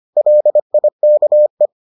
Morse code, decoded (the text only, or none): LIKE